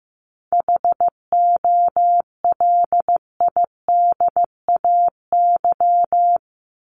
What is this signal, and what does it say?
Morse code, heard as a keyed tone, about 15 wpm, 700 Hz. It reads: HOLIDAY
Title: Top 1501-1750 Words 15wpm